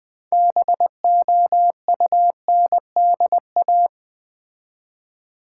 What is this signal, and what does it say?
Morse code, heard as a keyed tone, about 20 wpm, 700 Hz. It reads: BOUNDA
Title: Build A Word - 3 Letters At A Time - 8-Letter Words - 20wpm